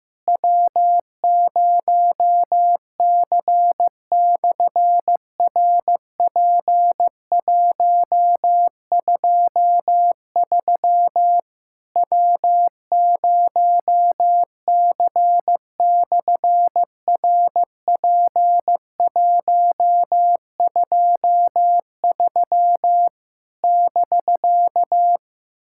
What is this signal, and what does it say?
Morse code, heard as a keyed tone, about 15 wpm, 700 Hz. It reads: W0C/RP123 W0C/RP123 <BK>
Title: SOTA QSO Practice 15wpm